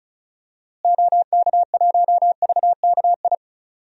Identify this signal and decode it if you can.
Morse code, heard as a keyed tone, about 36 wpm, 700 Hz. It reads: OK1VKI
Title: DXCC-100 Call Signs - 6 Characters - Repeated 2x - 35wpm